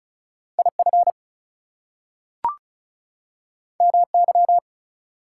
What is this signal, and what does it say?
Morse code, heard as a keyed tone, about 36 wpm, 700 Hz. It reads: IF MY
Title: Two Letter Words - 35wpm